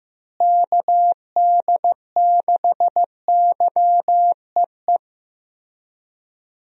Morse code, heard as a keyed tone, about 15 wpm, 700 Hz. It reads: KD6YEE
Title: USA Call Signs - 6 Characters - Repeated 3x - 15wpm